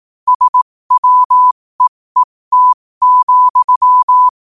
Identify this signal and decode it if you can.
Morse code, heard as a keyed tone, about 18 wpm, spaced as for 15 wpm, 1000 Hz. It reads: SWEET,